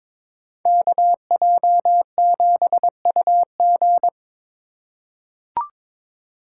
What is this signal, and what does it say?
Morse code, heard as a keyed tone, about 22 wpm, 700 Hz. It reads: KJ7UG
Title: USA Call Signs - 2x Character Spacing - 22wpm